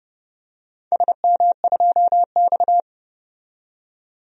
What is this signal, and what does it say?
Morse code, heard as a keyed tone, about 30 wpm, 700 Hz. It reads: SM2X
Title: DXCC-100 Call Signs - 4 Characters - 30wpm